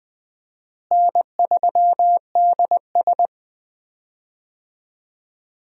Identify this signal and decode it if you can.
Morse code, heard as a keyed tone, about 20 wpm, 700 Hz. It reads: N3DS
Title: USA Call Signs - 4 Characters - Repeated 2x - 20wpm